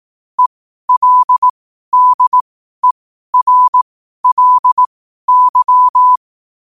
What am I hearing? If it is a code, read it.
Morse code, heard as a keyed tone, about 18 wpm, spaced as for 12 wpm, 1000 Hz. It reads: ELDERLY
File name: qotdfiles / 12wpm